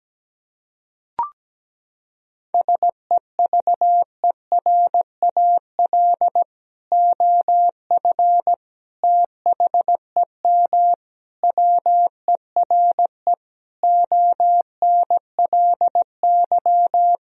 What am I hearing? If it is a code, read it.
Morse code, heard as a keyed tone, about 17 wpm, 700 Hz. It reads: SEVERAL OF THEM WERE ONLY
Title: Sentences from Top 1000 Words - Review Part 2 - 17wpm